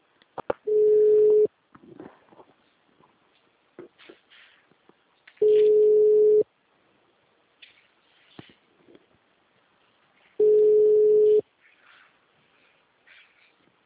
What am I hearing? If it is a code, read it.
Ringback tone (Europe, 425 Hz).